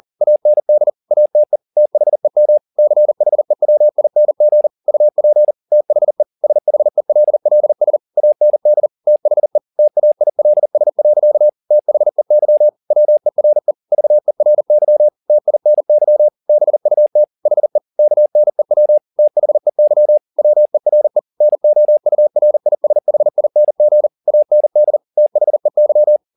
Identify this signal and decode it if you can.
Morse code, heard as a keyed tone, about 40 wpm, 600 Hz. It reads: AND ATE THEM CHEWING UP THE SHELLS AND THE TAILS. THEY WERE VERY TINY BUT HE KNEW THEY WERE NOURISHING AND THEY
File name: the old man and the sea 2 40wpm